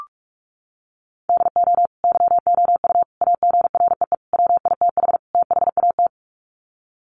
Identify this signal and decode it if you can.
Morse code, heard as a keyed tone, about 44 wpm, 700 Hz. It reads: DO YOU AGREE WITH THAT
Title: Sets of 5 Words Encore 45wpm